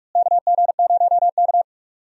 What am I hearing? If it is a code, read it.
Morse code, heard as a keyed tone, about 48 wpm, 700 Hz. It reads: KG0K
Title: USA Call Signs - 4 Characters - Repeated 2x - 45wpm